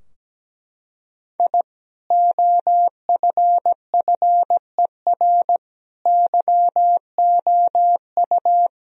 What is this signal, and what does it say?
Morse code, heard as a keyed tone, about 17 wpm, 700 Hz. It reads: I OFFER YOU
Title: Sentences from Top 1000 Words - Review Part 4 - 17wpm